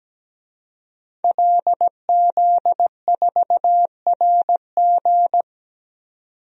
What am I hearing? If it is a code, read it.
Morse code, heard as a keyed tone, about 17 wpm, 700 Hz. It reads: LZ4RG
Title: DXCC-100 Call Signs - 5 Characters - 17wpm